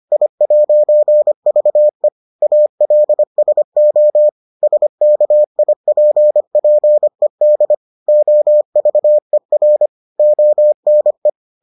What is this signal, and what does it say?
Morse code, heard as a keyed tone, about 25 wpm, 600 Hz. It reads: I'VE ALSO SKIPPED OVER ONE